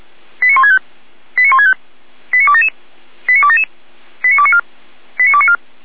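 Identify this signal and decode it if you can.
Five-tone selective call (ZVEI), heard as quick runs of five tones, tones of about 70 ms. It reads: 89165 89165 89277 89277 89294 89294
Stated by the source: ZVEI five-tone call 89042 received by SDR.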